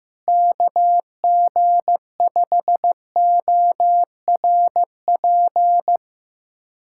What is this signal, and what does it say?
Morse code, heard as a keyed tone, about 16 wpm, 700 Hz. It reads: KG5ORP
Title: USA Callsign DE Callsign - 6 Characters - 15wpm